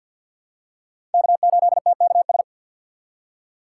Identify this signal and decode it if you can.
Morse code, heard as a keyed tone, about 51 wpm, 700 Hz. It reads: K8TKS
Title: USA Call Signs - 5 Characters - 4x Character Spacing - 50wpm